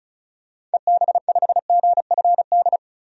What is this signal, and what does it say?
Morse code, heard as a keyed tone, about 34 wpm, 700 Hz. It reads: EB5GFD
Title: DXCC-100 Call Signs - 6 Characters - 35wpm